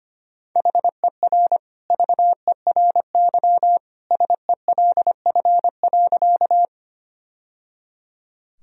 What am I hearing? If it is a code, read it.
Morse code, heard as a keyed tone, about 25 wpm, 700 Hz. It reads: HER VERY SELF.
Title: Sentences from Top 1000 Words - Review Part 1 - 25wpm